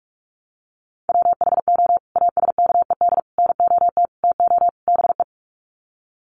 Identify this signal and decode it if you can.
Morse code, heard as a keyed tone, about 44 wpm, 700 Hz. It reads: WHO ASKED NOT TO BE
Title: Sets of 5 Words Encore2 45wpm